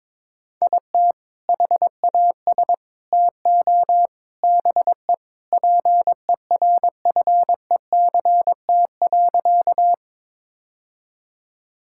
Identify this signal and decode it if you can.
Morse code, heard as a keyed tone, about 22 wpm, 700 Hz. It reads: IT HAS TO BE PERFECT.
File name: practice / 22wpm / Sentences with 3-Word Phrases - 22wpm